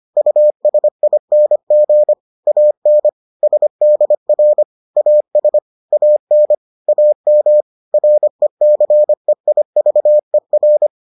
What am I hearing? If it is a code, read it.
Morse code, heard as a keyed tone, about 25 wpm, 600 Hz. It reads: USING AN SDR AS AN AM RECEIVER